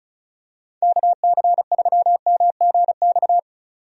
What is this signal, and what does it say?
Morse code, heard as a keyed tone, about 34 wpm, 700 Hz. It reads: KC3MGX
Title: USA Call Signs - 6 Characters - 35wpm